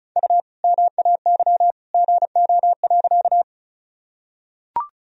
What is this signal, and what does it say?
Morse code, heard as a keyed tone, about 34 wpm, 700 Hz. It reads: U MAY GO.
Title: Sentences from Top 100 Words and Abbreviations 35wpm